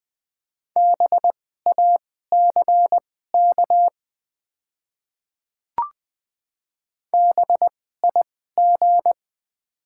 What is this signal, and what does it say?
Morse code, heard as a keyed tone, about 20 wpm, spaced as for 14 wpm, 700 Hz. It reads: BACK BIG
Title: 2x Character Spacing - Top 300 Words - 20wpm